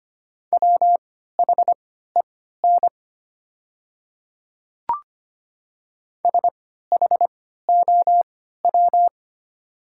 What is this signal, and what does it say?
Morse code, heard as a keyed tone, about 25 wpm, spaced as for 14 wpm, 690 Hz. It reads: WHEN SHOW
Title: 3x Character Spacing - Top 500 Words - 25wpm